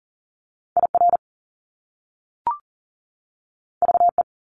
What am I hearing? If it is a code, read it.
Morse code, heard as a keyed tone, about 43 wpm, 700 Hz. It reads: IR VE
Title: Top Two Letter Combinations in Words 40wpm